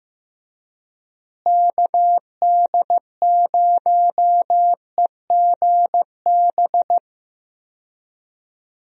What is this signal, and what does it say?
Morse code, heard as a keyed tone, about 15 wpm, 700 Hz. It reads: KD0EGB
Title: USA Call Signs - 6 Characters - Repeated 2x - 15wpm